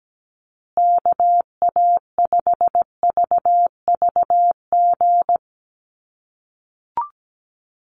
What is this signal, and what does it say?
Morse code, heard as a keyed tone, about 17 wpm, 700 Hz. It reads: KA5VVG
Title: USA Call Signs Repeated 3x 17wpm